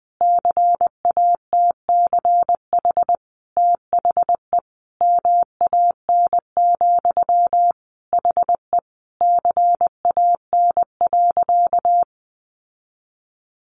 Morse code, heard as a keyed tone, about 20 wpm, 700 Hz. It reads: CATCH THE MAN, HE CAN.
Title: Sentences from Top 1500 Words 20wpm